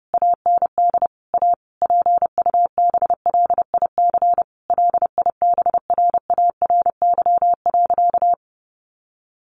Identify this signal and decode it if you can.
Morse code, heard as a keyed tone, about 30 wpm, 700 Hz. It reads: AND A PUBLIC LIBRARY.
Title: Sentences from Top 1300 Words 30wpm